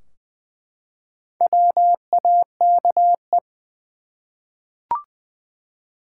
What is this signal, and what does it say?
Morse code, heard as a keyed tone, about 20 wpm, 700 Hz. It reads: WAKE
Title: Four Letter Words - 20wpm